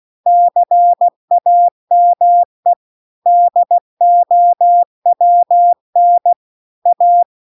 Morse code, heard as a keyed tone, about 16 wpm, 700 Hz. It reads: CAME DOWN A